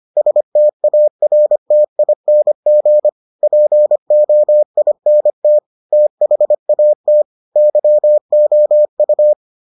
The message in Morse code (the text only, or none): STARTING POINT THAT YOU